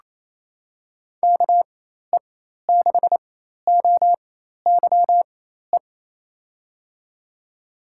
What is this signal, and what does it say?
Morse code, heard as a keyed tone, about 28 wpm, spaced as for 13 wpm, 700 Hz. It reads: KE6OYE
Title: USA Call Signs - 4x Character Spacing - 28wpm